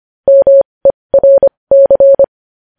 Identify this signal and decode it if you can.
Morse code, heard as a keyed tone, about 25 wpm, spaced as for 20 wpm, 550 Hz. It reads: MERC
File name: podcast / News headlines-1x-25x20